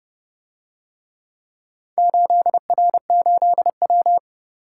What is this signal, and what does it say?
Morse code, heard as a keyed tone, about 31 wpm, 700 Hz. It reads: 8R8W